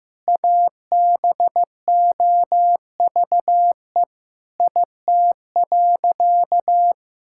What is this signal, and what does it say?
Morse code, heard as a keyed tone, about 15 wpm, 700 Hz. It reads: ABOVE IT.
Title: Sentences from Top 1600 Words 15wpm